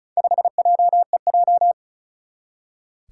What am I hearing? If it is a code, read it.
Morse code, heard as a keyed tone, about 36 wpm, 700 Hz. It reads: 5JEJ
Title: Four Challenging Letter-Number Combinations 35wpm